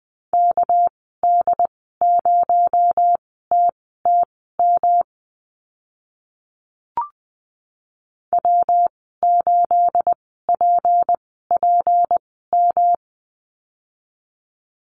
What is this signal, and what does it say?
Morse code, heard as a keyed tone, about 20 wpm, spaced as for 15 wpm, 700 Hz. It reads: KD0TTM W8PPM
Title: 231 Lesson 37g - Letter x - Call Signs 20wpm